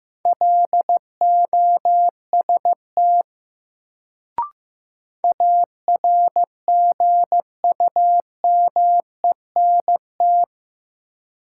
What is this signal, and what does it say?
Morse code, heard as a keyed tone, about 15 wpm, 700 Hz. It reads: LOST ARGUMENT
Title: Sentences from Top 1400 Words - New Word Drill - 15wpm